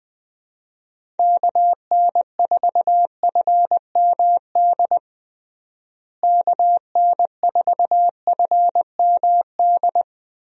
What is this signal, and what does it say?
Morse code, heard as a keyed tone, about 20 wpm, 700 Hz. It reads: KN4FMD KN4FMD
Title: USA Call Signs - 6 Characters - Repeated 3x - 20wpm